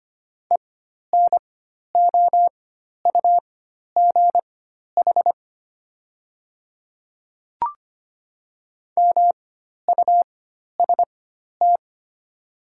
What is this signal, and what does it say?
Morse code, heard as a keyed tone, about 25 wpm, spaced as for 12 wpm, 700 Hz. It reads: ENOUGH MUST
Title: 4x Character Spacing - Top 500 Words - 25wpm